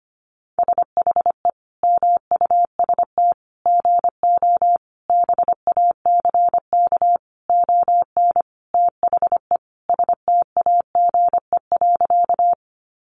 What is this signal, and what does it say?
Morse code, heard as a keyed tone, about 25 wpm, 700 Hz. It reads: SHE MUST GO BACK ON THE STAGE.
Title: Sentences from Top 1200 Words 25wpm